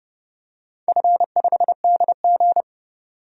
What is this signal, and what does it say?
Morse code, heard as a keyed tone, about 30 wpm, 700 Hz. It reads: F5DG
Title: DXCC-100 Call Signs - 4 Characters - 30wpm